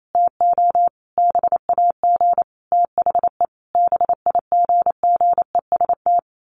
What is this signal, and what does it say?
Morse code, heard as a keyed tone, about 28 wpm, 700 Hz. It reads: TO BAG THE BIGGEST
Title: A Sound Of Thunder 28wpm